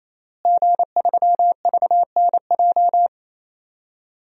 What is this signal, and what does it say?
Morse code, heard as a keyed tone, about 27 wpm, 700 Hz. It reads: G3VNJ
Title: DXCC-100 Call Signs - 5 Characters - Repeated 3x - 28wpm